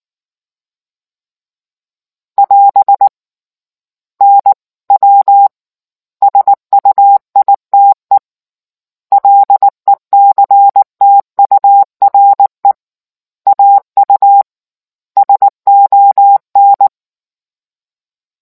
Morse code, heard as a keyed tone, about 19 wpm, 800 Hz. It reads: <AS> N W SUITE LECTURE AU SON